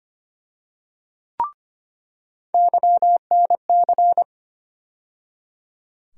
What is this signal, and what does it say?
Morse code, heard as a keyed tone, about 25 wpm, 700 Hz. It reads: YNC
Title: USA Call Sign Postfixes - 3 Letters 25wpm